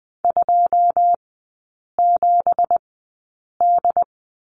Morse code, heard as a keyed tone, about 20 wpm, spaced as for 8 wpm, 690 Hz. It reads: 27D
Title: 166 Lesson 27e - Number 7 - Three Characters 20wpm